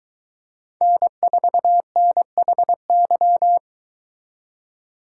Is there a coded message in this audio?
Yes: Morse code from 0.8 to 3.6 s, heard as a keyed tone, about 22 wpm, 700 Hz. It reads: N4NHY